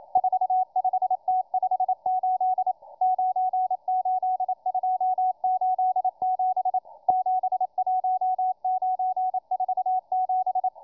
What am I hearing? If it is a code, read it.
Morse code, heard as a keyed tone, about 27 wpm, 740 Hz, with radio noise. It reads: 45T58 98287 71947